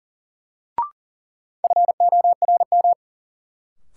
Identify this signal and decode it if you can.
Morse code, heard as a keyed tone, about 40 wpm, 700 Hz. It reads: FORM
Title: Sets of 4 Words - New Word Drill - 40wpm